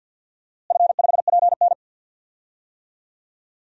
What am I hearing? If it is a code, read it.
Morse code, heard as a keyed tone, about 51 wpm, 700 Hz. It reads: F5PN
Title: DXCC-100 Call Signs - 4 Characters - Repeated 2x - 50wpm